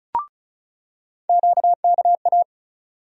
Morse code, heard as a keyed tone, about 36 wpm, 700 Hz. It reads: QKA
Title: USA Call Sign Postfixes - 3 Letters 35wpm